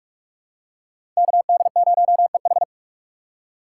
Morse code, heard as a keyed tone, about 41 wpm, 700 Hz. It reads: KD0EH
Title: USA Call Signs - 5 Characters - 6x Character Spacing - 45wpm